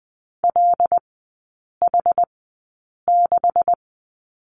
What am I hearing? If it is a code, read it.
Morse code, heard as a keyed tone, about 20 wpm, spaced as for 8 wpm, 700 Hz. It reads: LH6
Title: 145 Lesson 24e - Letter y - Three Characters 20wpm